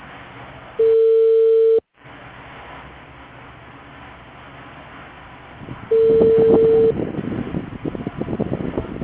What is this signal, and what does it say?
Ringback tone (China, 450 Hz).